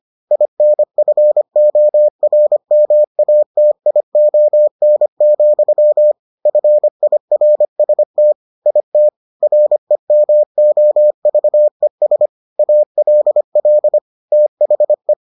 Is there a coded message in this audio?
Yes: Morse code from 0.3 to 15.1 s, heard as a keyed tone, about 25 wpm, 600 Hz. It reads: INFORMATION, FIRST IT REMOVES ALL THE